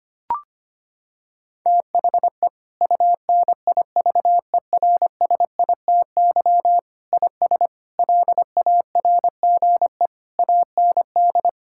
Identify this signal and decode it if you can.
Morse code, heard as a keyed tone, about 25 wpm, 700 Hz. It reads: THE UNIVERSITY IS LARGE AND
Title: Sentences from Top 1000 Words - Review Part 5 - 25wpm